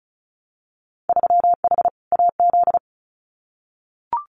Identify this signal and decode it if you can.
Morse code, heard as a keyed tone, about 34 wpm, 700 Hz. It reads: 3H AZ
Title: Winter Field Day Exchanges Encore 35wpm